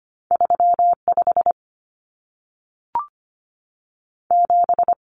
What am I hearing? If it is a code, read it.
Morse code, heard as a keyed tone, about 25 wpm, 700 Hz. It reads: 3 5 7
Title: Numbers - 2-Digit 25wpm